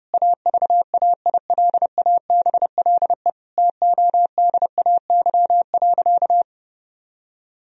Morse code, heard as a keyed tone, about 28 wpm, 700 Hz. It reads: AVAILABLE TODAY.
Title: Sentences from Top 1000 Words - Review Part 3 - 30wpm